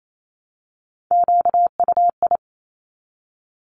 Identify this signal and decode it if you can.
Morse code, heard as a keyed tone, about 28 wpm, 700 Hz. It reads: QUI
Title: Top Three Letter Combinations in Words 28wpm